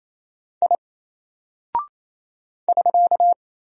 Morse code, heard as a keyed tone, about 27 wpm, 700 Hz. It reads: I <SK>
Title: Single Character (Letter:Number:Punctuation:Prosign) 28wpm